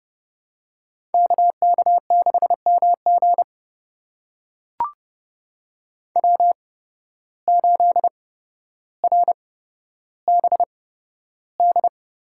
Morse code, heard as a keyed tone, about 32 wpm, spaced as for 13 wpm, 700 Hz. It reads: KK6MG W8RBD
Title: USA Call Signs - 5 Characters - 8x Character Spacing - 30wpm